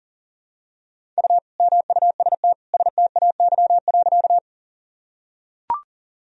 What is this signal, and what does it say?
Morse code, heard as a keyed tone, about 40 wpm, 700 Hz. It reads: U MUST STAY.
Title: Sentences from Top 500 Words and Abbreviations 40wpm